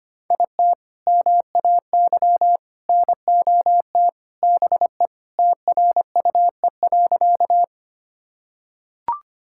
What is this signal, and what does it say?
Morse code, heard as a keyed tone, about 25 wpm, 700 Hz. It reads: IT MAY NOT BE TRUE.
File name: practice / 25wpm / Sentences with 3-Word Phrases - 25wpm